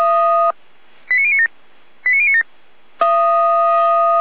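Five-tone selective call (ZVEI), heard as quick runs of five tones, tones of about 80 ms. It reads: 89097 89097
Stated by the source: ZVEI five-tone selective call received by SDR.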